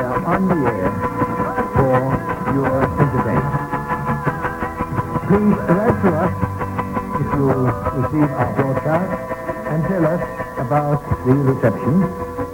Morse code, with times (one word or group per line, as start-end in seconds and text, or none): none